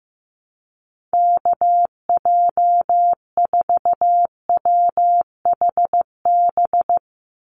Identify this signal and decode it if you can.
Morse code, heard as a keyed tone, about 15 wpm, 700 Hz. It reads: KJ4WHB
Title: USA Call Signs 15wpm